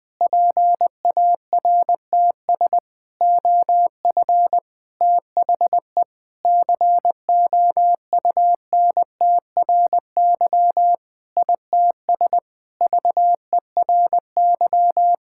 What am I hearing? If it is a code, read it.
Morse code, heard as a keyed tone, about 20 wpm, 700 Hz. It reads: PARTS OF THE COUNTRY ITS VERY
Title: Sentences with 5-Word Phrases - Spoken Beginning - 20wpm